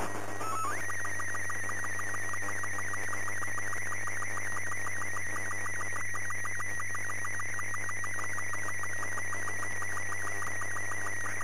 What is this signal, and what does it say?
SSTV (slow-scan television), Robot 36.
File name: ISS / na1iss 15oct08 sstv